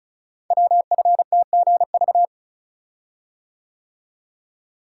Morse code, heard as a keyed tone, about 36 wpm, 690 Hz. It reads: WFTGV
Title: Code Groups - 5 Letters - 35wpm